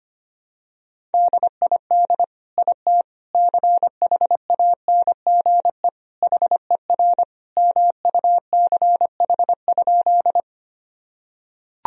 Morse code, heard as a keyed tone, about 25 wpm, 700 Hz. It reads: DID IT CHANGE HER MUCH?
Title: Sentences from Top 1000 Words - Review Part 1 - 25wpm